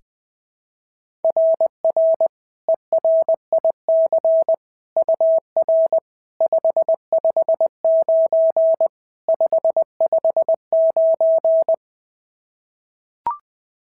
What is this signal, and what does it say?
Morse code, heard as a keyed tone, about 21 wpm, 690 Hz. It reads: RR ERIC UR 559 559
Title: SOTA QSO Practice 20wpm